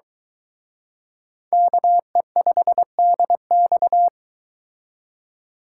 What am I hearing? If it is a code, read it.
Morse code, heard as a keyed tone, about 23 wpm, 700 Hz. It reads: KE5DX